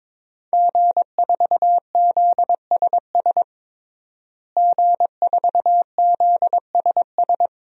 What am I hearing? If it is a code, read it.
Morse code, heard as a keyed tone, about 22 wpm, 700 Hz. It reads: G4ZSS G4ZSS
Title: DXCC-100 Call Signs - 5 Characters - Repeated 2x - 22wpm